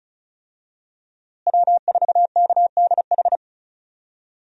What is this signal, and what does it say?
Morse code, heard as a keyed tone, about 36 wpm, 700 Hz. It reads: W4KDH
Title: USA Call Signs - 5 Characters - Repeated 2x - 35wpm